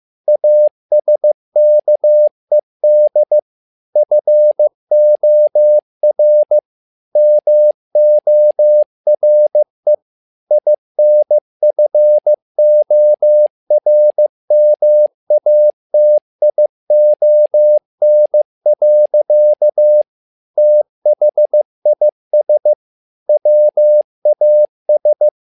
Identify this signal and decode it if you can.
Morse code, heard as a keyed tone, about 15 wpm, 600 Hz. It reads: ASKED FOR MORE INFORMATION. THIS WAS
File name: learning center / tower trouble / tower trouble 15wpm